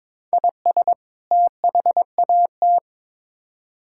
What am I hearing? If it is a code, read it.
Morse code, heard as a keyed tone, about 21 wpm, 700 Hz. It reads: IS THAT
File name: Build A Phrase - 2-Word Phrases - 22wpm